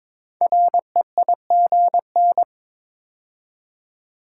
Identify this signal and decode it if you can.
Morse code, heard as a keyed tone, about 22 wpm, 700 Hz. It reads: REIGN
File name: 5 Letter Words - First 3 Letters Given - 22wpm